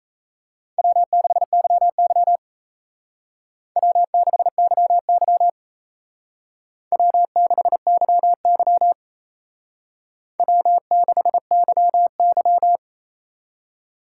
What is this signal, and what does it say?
Morse code, heard as a keyed tone, about 34 wpm, 700 Hz. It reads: W6YY W6YY W6YY W6YY
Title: Speed Racing - USA Call Signs 28wpm